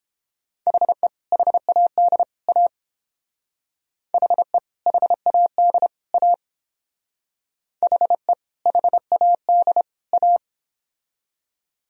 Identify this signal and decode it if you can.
Morse code, heard as a keyed tone, about 31 wpm, 700 Hz. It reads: HE HAD A HE HAD A HE HAD A